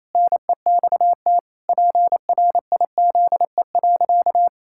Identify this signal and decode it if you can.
Morse code, heard as a keyed tone, about 27 wpm, 700 Hz. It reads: NEXT PRIZE.
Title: Pangram Challenge v2 - 28wpm